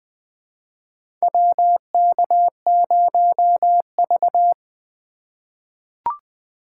Morse code, heard as a keyed tone, about 20 wpm, 700 Hz. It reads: WK0V
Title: USA Call Signs - 4 Characters - 4x Character Spacing - 20wpm